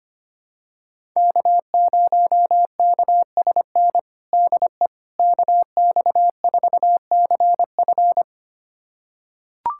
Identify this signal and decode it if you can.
Morse code, heard as a keyed tone, about 25 wpm, 700 Hz. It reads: K0KSN DE KX4CF
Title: USA Callsign DE Callsign 25wpm